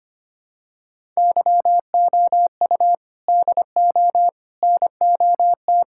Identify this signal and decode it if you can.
Morse code, heard as a keyed tone, about 25 wpm, 700 Hz. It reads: YOU DO NOT
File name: practice / 25wpm / Sentences with 4-Word Phrases - 25wpm